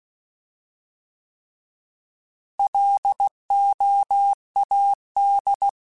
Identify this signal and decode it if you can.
Morse code, heard as a keyed tone, about 16 wpm, 800 Hz. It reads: LOAD
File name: Alphabetical / Alphabetical mp3s / Ls @15